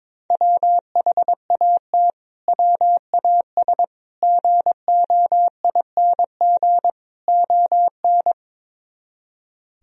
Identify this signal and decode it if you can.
Morse code, heard as a keyed tone, about 22 wpm, 700 Hz. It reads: WHAT WAS GOING ON
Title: Speed-Racing - Sets of 4 Words 22wpm